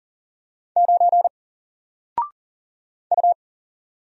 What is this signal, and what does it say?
Morse code, heard as a keyed tone, about 38 wpm, 700 Hz. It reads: 9 U
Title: Single Letter-Number Alt 40wpm